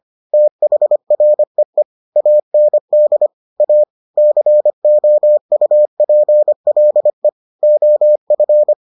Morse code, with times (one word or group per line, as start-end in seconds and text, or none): 0.3-1.8 THREE
2.2-3.3 AND
3.6-3.8 A
4.2-7.3 COUPLE
7.6-8.7 OF